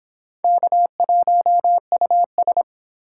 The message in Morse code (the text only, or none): K1US